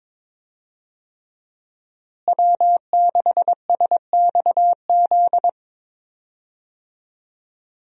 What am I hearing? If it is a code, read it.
Morse code, heard as a keyed tone, about 22 wpm, 700 Hz. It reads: W6SXZ
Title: USA Call Signs - 5 Characters - Repeated 3x - 22wpm